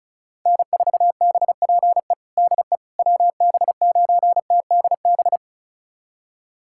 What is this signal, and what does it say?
Morse code, heard as a keyed tone, about 36 wpm, 700 Hz. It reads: N4BPE DE WB9TDB
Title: USA Callsign DE Callsign 35wpm